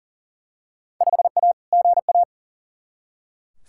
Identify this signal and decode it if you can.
Morse code, heard as a keyed tone, about 40 wpm, 700 Hz. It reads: 5A GA
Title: Field Day Exchanges Encore - 40wpm